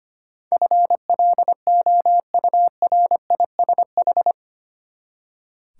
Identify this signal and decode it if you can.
Morse code, heard as a keyed tone, about 25 wpm, 700 Hz. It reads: FLOURISH
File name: Eight Letter Words - 25wpm